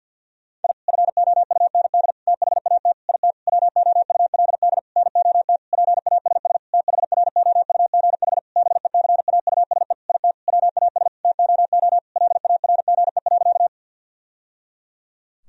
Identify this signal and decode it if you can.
Morse code, heard as a keyed tone, about 51 wpm, 700 Hz. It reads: I FOUND THAT IT WOULD NOT PASS THROUGH BECAUSE IT WAS TOO LARGE.
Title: Sentences from Top 1000 Words - Review Part 1 - 50wpm